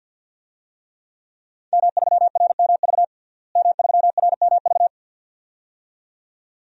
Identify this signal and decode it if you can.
Morse code, heard as a keyed tone, about 51 wpm, 700 Hz. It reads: M3RMV M3RMV
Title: DXCC-100 Call Signs - 5 Characters - Repeated 2x - 50wpm